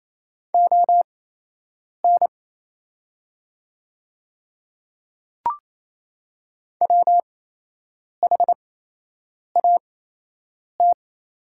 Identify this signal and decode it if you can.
Morse code, heard as a keyed tone, about 28 wpm, spaced as for 8 wpm, 700 Hz. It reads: ON WHAT
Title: 8x Character Spacing - Top 100 Words - 28wpm